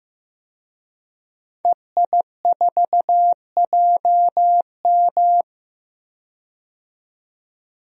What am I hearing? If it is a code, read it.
Morse code, heard as a keyed tone, about 15 wpm, 700 Hz. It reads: EI4JM
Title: DXCC-100 Call Signs - 5 Characters - 15wpm